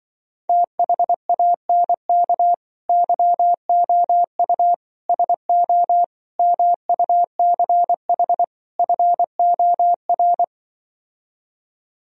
Morse code, heard as a keyed tone, about 24 wpm, 700 Hz. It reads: THANK YOU SO MUCH FOR
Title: Speed-Racing - Sets of 5 Words 20wpm